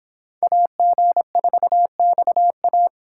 Morse code, heard as a keyed tone, about 26 wpm, 700 Hz. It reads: AG4XA